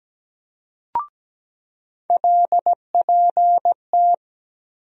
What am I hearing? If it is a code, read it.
Morse code, heard as a keyed tone, about 17 wpm, 700 Hz. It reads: LPT
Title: USA Call Sign Postfixes - 3 Letters 17wpm